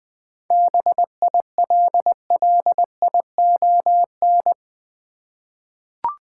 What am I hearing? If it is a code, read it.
Morse code, heard as a keyed tone, about 20 wpm, 700 Hz. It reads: BILLION
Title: 7 Letter Words - First 3 Letters Given - 20wpm